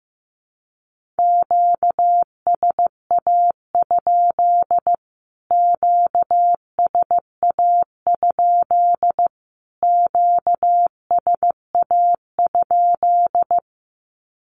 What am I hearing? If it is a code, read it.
Morse code, heard as a keyed tone, about 15 wpm, 700 Hz. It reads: QSA? QSA? QSA?
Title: Q-Codes Repeated 3x 15wpm